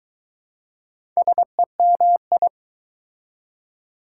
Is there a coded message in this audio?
Yes: Morse code from 1.2 to 2.5 s, heard as a keyed tone, about 20 wpm, 700 Hz. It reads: SEMI